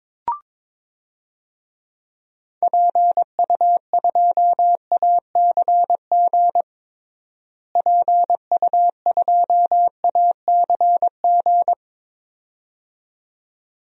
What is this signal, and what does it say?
Morse code, heard as a keyed tone, about 21 wpm, 700 Hz. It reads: PU2ACG PU2ACG
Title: DXCC-100 Call Signs - 6 Characters - Repeated 2x - 22wpm